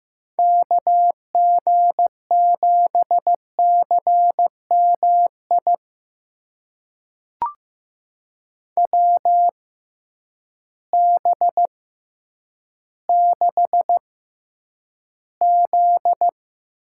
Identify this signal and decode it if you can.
Morse code, heard as a keyed tone, about 15 wpm, spaced as for 9 wpm, 700 Hz. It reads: KG7CMI WB6Z
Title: USA Call Signs - 6 Characters - 6x Character Spacing - 15wpm